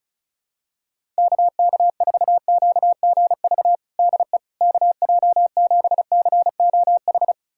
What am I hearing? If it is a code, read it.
Morse code, heard as a keyed tone, about 34 wpm, 700 Hz. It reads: KK4QGV DE KJ7COH
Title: USA Callsign DE Callsign 35wpm